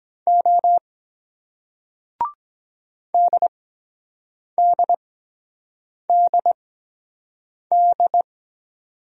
Morse code, heard as a keyed tone, about 22 wpm, spaced as for 7 wpm, 700 Hz. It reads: O DDDD